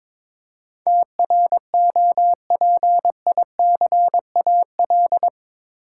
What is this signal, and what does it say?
Morse code, heard as a keyed tone, about 22 wpm, 700 Hz. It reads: TROPICAL
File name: practice / 22wpm / Eight Letter Words - 22wpm